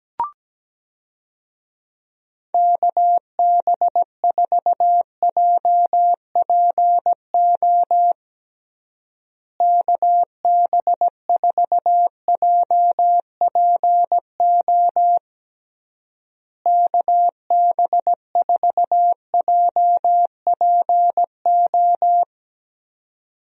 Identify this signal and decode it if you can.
Morse code, heard as a keyed tone, about 17 wpm, 700 Hz. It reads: KB4JPO KB4JPO KB4JPO
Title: USA Call Signs - 6 Characters - Repeated 3x - 17wpm